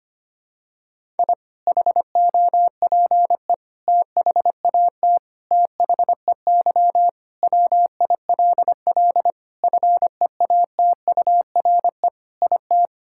Morse code, heard as a keyed tone, about 25 wpm, 700 Hz. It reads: I HOPE THAT THEY WILL FEATURE IT
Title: Sentences from Top 1000 Words - Review Part 5 - 25wpm